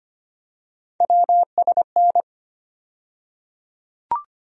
Morse code, heard as a keyed tone, about 25 wpm, 700 Hz. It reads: WSN
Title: USA Call Sign Postfixes - 3 Letters 25wpm